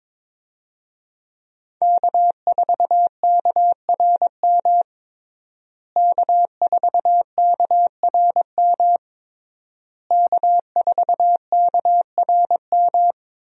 Morse code, heard as a keyed tone, about 21 wpm, 700 Hz. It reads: K4KRM K4KRM K4KRM
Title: USA Call Signs - 5 Characters - Repeated 3x - 22wpm